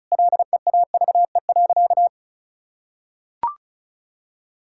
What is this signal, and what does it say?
Morse code, heard as a keyed tone, about 36 wpm, 700 Hz. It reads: LEAVE.
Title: Sentences from Top 1000 Words - Review Part 6 - 35wpm